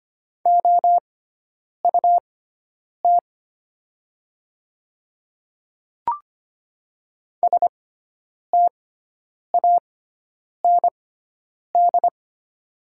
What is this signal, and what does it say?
Morse code, heard as a keyed tone, about 25 wpm, spaced as for 9 wpm, 700 Hz. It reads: OUT STAND
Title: 6x Character Spacing - Top 500 Words - 25wpm